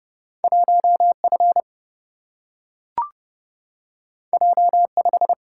Morse code, heard as a keyed tone, about 30 wpm, 700 Hz. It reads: 1F J5
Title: Single-Letter Single-Number 30wpm